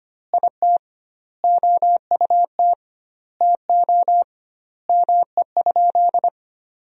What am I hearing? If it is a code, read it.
Morse code, heard as a keyed tone, about 25 wpm, 700 Hz. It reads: IT OUT TO ME?
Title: 2x Word-Spacing - Sentences from Top 200 Words - 25wpm